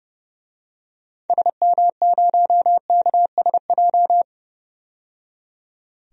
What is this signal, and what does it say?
Morse code, heard as a keyed tone, about 30 wpm, 700 Hz. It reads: SM0KSJ
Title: DXCC-100 Call Signs - 6 Characters - 30wpm